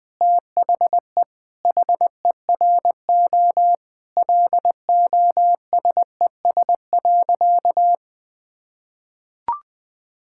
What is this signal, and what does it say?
Morse code, heard as a keyed tone, about 21 wpm, 700 Hz. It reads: THE HERO LOSES.
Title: Sentences with 6-Word Phrases - 20wpm